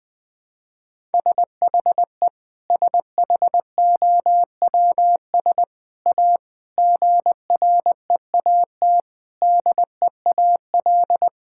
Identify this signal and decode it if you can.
Morse code, heard as a keyed tone, about 20 wpm, 700 Hz. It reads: SHE SHOWS A GREAT DEAL
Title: Sentences with 4-Word Phrases - Spoken Beginning - 20wpm